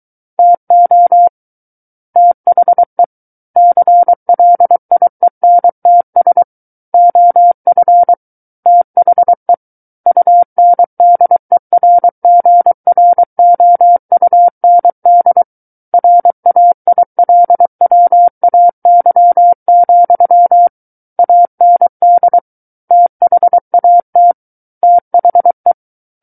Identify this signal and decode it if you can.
Morse code, heard as a keyed tone, about 23 wpm, 700 Hz. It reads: TO THE CLIENTS OF THE UNDERGROUND RAILWAY, AND THAT THE
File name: Book One / 14-Chapter 14 - 23 WPM